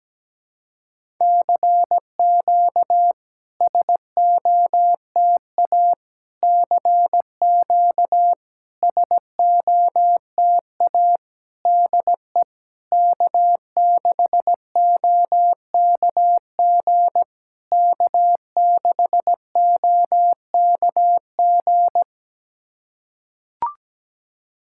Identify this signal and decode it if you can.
Morse code, heard as a keyed tone, about 17 wpm, 690 Hz. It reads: CQ SOTA CQ SOTA DE K6OKG K6OKG
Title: SOTA QSO Practice 17wpm